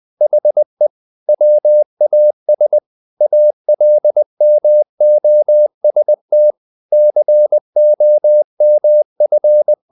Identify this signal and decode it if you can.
Morse code, heard as a keyed tone, about 20 wpm, 600 Hz. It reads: HE WAS ALMOST COMF
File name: the old man and the sea / the old man and the sea part 1 20wpm